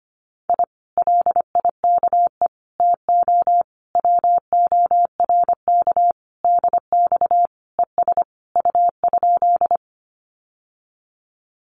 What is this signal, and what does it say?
Morse code, heard as a keyed tone, about 25 wpm, 700 Hz. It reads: I LIKE TO WORK DX ES U?
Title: Common QSO Phrases Encore 25wpm